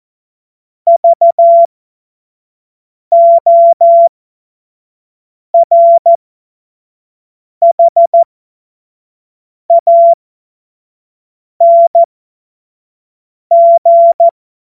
Morse code, heard as a keyed tone, about 14 wpm, spaced as for 5 wpm, 680 Hz. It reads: VORHANG
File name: Morsekurs / Witze / Witz 01. Juni 2021, 25 F70